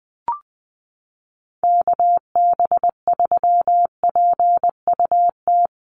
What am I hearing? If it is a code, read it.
Morse code, heard as a keyed tone, about 20 wpm, 700 Hz. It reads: KB3PUT
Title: USA Call Signs Repeated 3x 20wpm